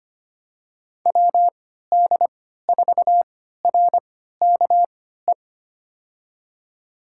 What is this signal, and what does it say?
Morse code, heard as a keyed tone, about 25 wpm, spaced as for 14 wpm, 700 Hz. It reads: WD4RKE
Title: USA Call Signs - 3x Character Spacing - 25wpm